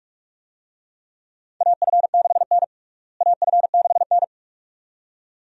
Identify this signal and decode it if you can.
Morse code, heard as a keyed tone, about 43 wpm, 700 Hz. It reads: AF6N AF6N
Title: USA Call Signs - 4 Characters - Repeated 2x - 45wpm